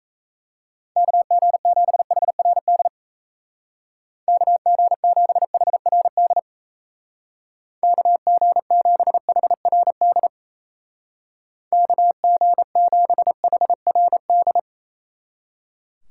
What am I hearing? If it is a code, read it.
Morse code, heard as a keyed tone, about 36 wpm, 700 Hz. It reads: KG7HRD KG7HRD KG7HRD KG7HRD